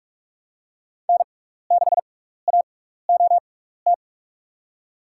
Morse code, heard as a keyed tone, about 44 wpm, spaced as for 15 wpm, 700 Hz. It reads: N6AOT
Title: USA Call Signs - 5 Characters - 6x Character Spacing - 45wpm